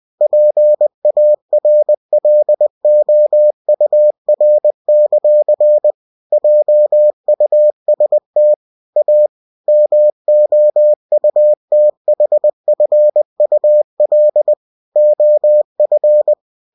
Morse code, heard as a keyed tone, about 20 wpm, 600 Hz. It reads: PARLOUR; JUST A MOUTHFUL OF
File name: cw-books / winnie-the-pooh / winnie-20wpm-page-0030